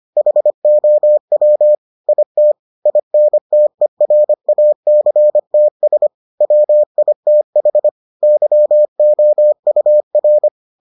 Morse code, heard as a keyed tone, about 25 wpm, 600 Hz. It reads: HOW IT INTERACTS WITH YOUR